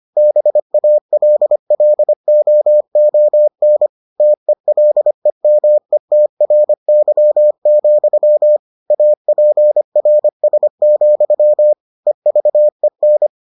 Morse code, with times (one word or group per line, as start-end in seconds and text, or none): 0.2-3.9 BALLOON
4.2-8.6 TELEMETRY,
8.9-11.7 APRS,
12.1-13.3 EVEN